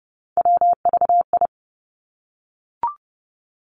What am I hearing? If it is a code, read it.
Morse code, heard as a keyed tone, about 30 wpm, 700 Hz. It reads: WVI
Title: Three Challenging Letter-Number Combinations 30wpm